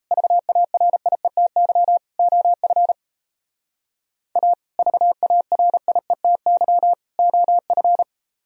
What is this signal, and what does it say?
Morse code, heard as a keyed tone, about 34 wpm, 700 Hz. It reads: VARIETY OF A VARIETY OF
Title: Speed-Racing - Sets of 3 Words 28wpm